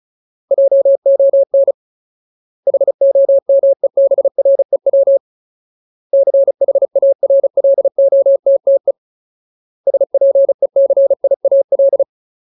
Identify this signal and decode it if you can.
Morse code, heard as a keyed tone, about 35 wpm, 550 Hz. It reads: JON HOMEBREW CHARLOTTE SPECIAL
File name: Ultimate Word List / 35WPM Ultimate